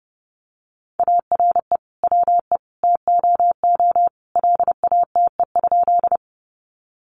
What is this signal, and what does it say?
Morse code, heard as a keyed tone, about 30 wpm, 700 Hz. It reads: ARE WE TOO LATE?
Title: Sentences from Top 300 Words 30wpm